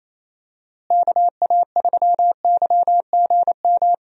Morse code, heard as a keyed tone, about 27 wpm, 700 Hz. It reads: KA3YGM
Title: USA Call Signs - 6 Characters - Repeated 3x - 28wpm